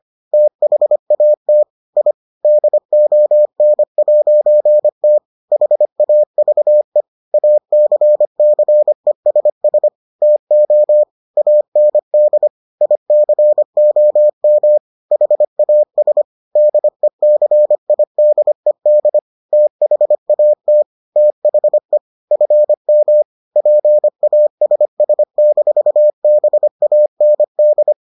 Morse code, heard as a keyed tone, about 25 wpm, 600 Hz. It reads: THAT I DON'T HAVE ACCESS TO AND ICOM HAS DECIDED THAT THE FM PASS-BAND